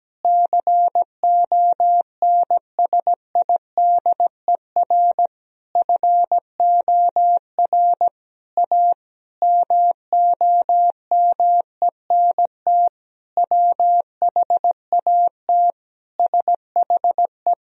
Morse code, heard as a keyed tone, about 17 wpm, 700 Hz. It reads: CONSIDER FOR A MOMENT WHAT SHE
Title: Sentences from Top 1000 Words - Review Part 4 - 17wpm